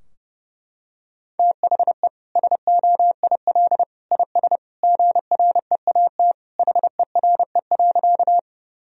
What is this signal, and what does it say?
Morse code, heard as a keyed tone, about 30 wpm, 700 Hz. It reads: THE SOIL IS GREAT HERE.
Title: Sentences from Top 1000 Words - Review Part 4 - 30wpm